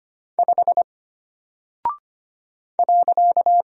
Morse code, heard as a keyed tone, about 25 wpm, 700 Hz. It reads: 5 .
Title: Single Character (Letter:Number:Punctuation:Prosign) Alt 25wpm